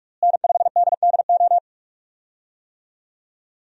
Morse code, heard as a keyed tone, about 46 wpm, 700 Hz. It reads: N5DDO
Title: USA Call Signs - 5 Characters - Repeated 3x - 45wpm